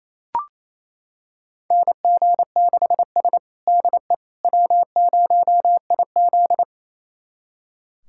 Morse code, heard as a keyed tone, about 28 wpm, 700 Hz. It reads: NG6S DE W0IZ
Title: USA Callsign DE Callsign - 4 Characters - 28wpm